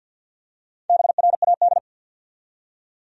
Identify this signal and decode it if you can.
Morse code, heard as a keyed tone, about 53 wpm, 700 Hz. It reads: BRAD
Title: USA Common Names - Max 5 Letters - 50wpm